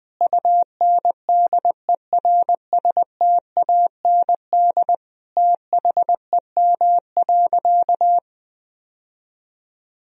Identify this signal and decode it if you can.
Morse code, heard as a keyed tone, about 21 wpm, 700 Hz. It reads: UNDERSTAND THEM.
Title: Sentences from Top 1000 Words - Review Part 2 - 20wpm